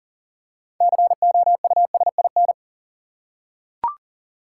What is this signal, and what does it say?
Morse code, heard as a keyed tone, about 40 wpm, 700 Hz. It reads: COUSIN
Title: Six Letter Words - 40wpm